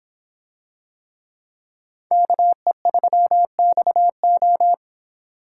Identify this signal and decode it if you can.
Morse code, heard as a keyed tone, about 26 wpm, 700 Hz. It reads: KE3XO